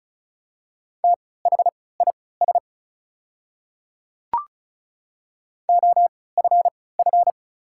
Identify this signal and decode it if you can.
Morse code, heard as a keyed tone, about 36 wpm, spaced as for 20 wpm, 700 Hz. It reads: THIS OFF
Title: 3x Character Spacing - Top 200 Words - 35wpm